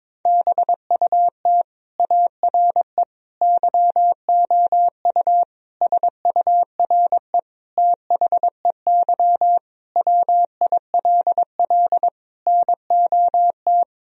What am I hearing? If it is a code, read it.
Morse code, heard as a keyed tone, about 22 wpm, 700 Hz. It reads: BUT ARE YOU SURE THEY WILL NOT
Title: Sentences from Top 1000 Words - Review Part 1 - 22wpm